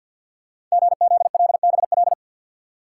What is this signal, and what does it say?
Morse code, heard as a keyed tone, about 48 wpm, 700 Hz. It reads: GZLBL
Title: Code Groups - 5 Letters - 50wpm